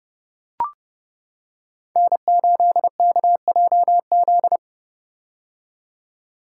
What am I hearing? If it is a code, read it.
Morse code, heard as a keyed tone, about 28 wpm, 700 Hz. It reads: N8KJZ
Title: Code Groups - 5 Letter:Numbers - 30wpm